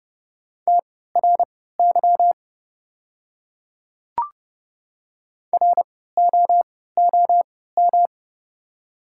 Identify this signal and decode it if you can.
Morse code, heard as a keyed tone, about 30 wpm, spaced as for 17 wpm, 690 Hz. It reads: TRY ROOM
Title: 3x Character Spacing - Top 500 Words - 30wpm